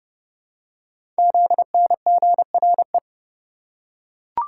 Morse code, heard as a keyed tone, about 29 wpm, 700 Hz. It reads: ZNGRE
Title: Code Groups - 5 Letters - 30wpm